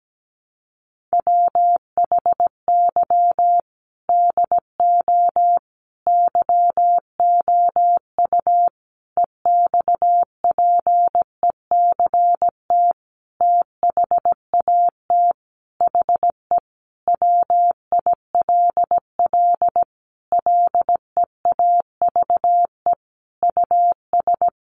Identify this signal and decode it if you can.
Morse code, heard as a keyed tone, about 17 wpm, 700 Hz. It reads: WHY DO YOU EXPECT THAT HE WILL LEAVE US
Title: Sentences from Top 1100 Words 17wpm